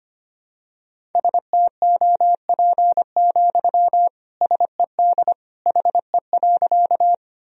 Morse code, heard as a keyed tone, about 25 wpm, 700 Hz. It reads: STOP, SED HE.
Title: Sentences from Top 300 Words and Abbreviations 25wpm